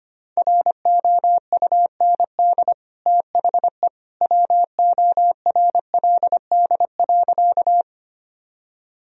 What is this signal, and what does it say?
Morse code, heard as a keyed tone, about 25 wpm, 700 Hz. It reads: ROUND THE WORLD.
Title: Sentences from Top 1000 Words - Review Part 1 - 25wpm